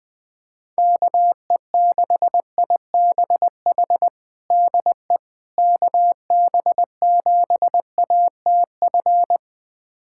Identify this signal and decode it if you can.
Morse code, heard as a keyed tone, about 21 wpm, 700 Hz. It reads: KE6IBH DE KB7ATF
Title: USA Callsign DE Callsign - 6 Characters - 20wpm